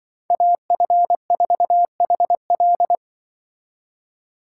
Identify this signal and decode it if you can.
Morse code, heard as a keyed tone, about 24 wpm, 700 Hz. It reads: AF4HL
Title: Speed Racing - USA 2x2 Call Signs 20wpm